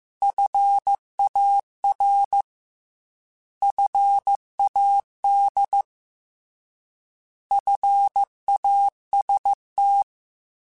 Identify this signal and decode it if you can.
Morse code, heard as a keyed tone, about 15 wpm, 800 Hz. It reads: FAR FAD FAST
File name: Alphabetical / Alphabetical mp3s / Fs @13